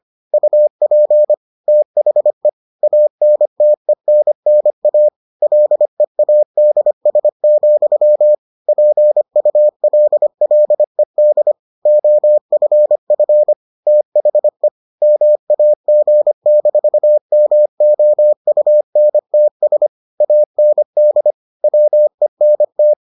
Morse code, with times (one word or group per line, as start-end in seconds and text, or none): 0.3-1.3 UP
1.7-2.5 THE
2.8-5.1 ANTENNA
5.4-8.3 LEADS,
8.7-11.5 PULLED
11.9-13.5 OFF
13.9-14.7 THE
15.0-19.9 MAG-MOUNTS
20.2-21.3 AND
21.6-22.9 WENT